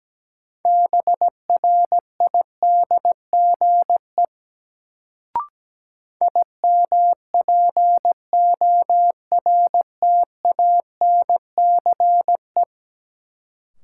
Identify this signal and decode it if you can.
Morse code, heard as a keyed tone, about 17 wpm, 700 Hz. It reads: BRIDGE IMPORTANCE
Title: Sentences from Top 1200 Words - New Word Drill - 17wpm